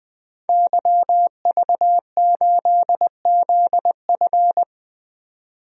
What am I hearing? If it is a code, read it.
Morse code, heard as a keyed tone, about 20 wpm, 700 Hz. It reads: YV8ZF
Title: DXCC-100 Call Signs - 5 Characters - 20wpm